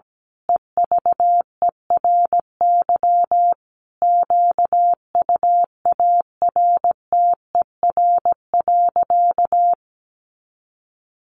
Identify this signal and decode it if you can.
Morse code, heard as a keyed tone, about 17 wpm, 700 Hz. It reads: EVERY QUARTER.
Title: Sentences from Top 1500 Words 17wpm